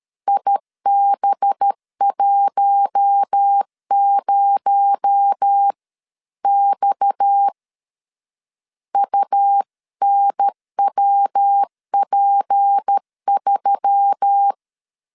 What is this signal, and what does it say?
Morse code, heard as a keyed tone, about 13 wpm, 800 Hz. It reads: IB10X UNWP3
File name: CW / DET / 1 6 1500C S A 12wpm 3440